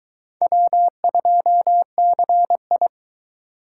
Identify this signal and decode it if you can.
Morse code, heard as a keyed tone, about 22 wpm, 700 Hz. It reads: W2CI